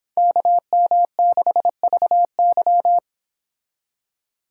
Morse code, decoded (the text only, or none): KM6VY